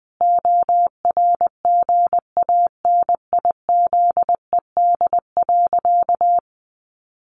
Morse code, heard as a keyed tone, about 20 wpm, 700 Hz. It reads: ORGANIZED.
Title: Sentences from Top 1300 Words 20wpm